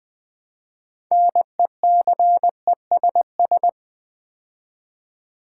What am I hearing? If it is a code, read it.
Morse code, heard as a keyed tone, about 20 wpm, 700 Hz. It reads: NECESS
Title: Build A Word - 3 Letters At A Time - 9-Letter Words - 20wpm